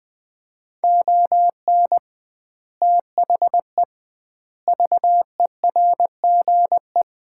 Morse code, heard as a keyed tone, about 19 wpm, 700 Hz. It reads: ON THE VERGE
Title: 2x Word-Spacing - Sets of 4 Words - 20wpm